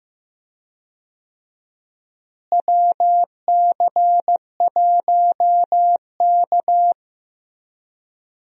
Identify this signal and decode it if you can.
Morse code, heard as a keyed tone, about 15 wpm, 700 Hz. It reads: WC1K
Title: USA Call Signs - 4 Characters - Repeated 2x - 15wpm